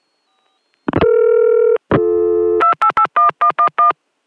DTMF, heard as telephone touch tones, tones of about 90 ms. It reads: 2881111